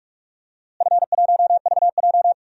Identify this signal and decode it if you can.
Morse code, heard as a keyed tone, about 41 wpm, 700 Hz. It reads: F1VJ
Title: DXCC-100 Call Signs - 4 Characters - Repeated 3x - 45wpm